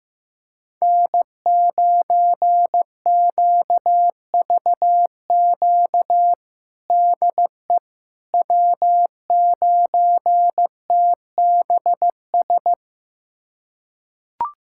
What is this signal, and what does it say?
Morse code, heard as a keyed tone, about 15 wpm, 700 Hz. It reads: N9QVQ DE W9TBS
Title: USA Callsign DE Callsign - 5 Characters - 15wpm